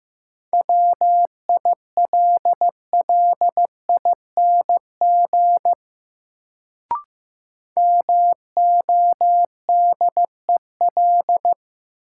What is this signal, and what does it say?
Morse code, heard as a keyed tone, about 15 wpm, 700 Hz. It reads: WILLING MODEL
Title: Sentences from Top 1000 Words - New Word Drill - 15wpm